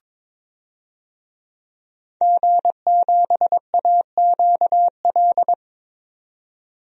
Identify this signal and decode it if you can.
Morse code, heard as a keyed tone, about 22 wpm, 700 Hz. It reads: G7AQL
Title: DXCC-100 Call Signs - 5 Characters - 22wpm